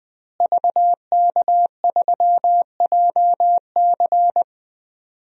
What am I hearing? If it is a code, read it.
Morse code, heard as a keyed tone, about 21 wpm, 700 Hz. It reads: VK3JC
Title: DXCC-100 Call Signs - 5 Characters - Repeated 2x - 20wpm